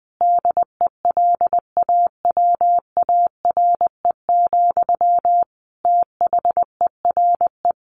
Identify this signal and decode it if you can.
Morse code, heard as a keyed tone, about 20 wpm, 700 Hz. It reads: DELAWARE, THERE
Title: A Sound Of Thunder 20wpm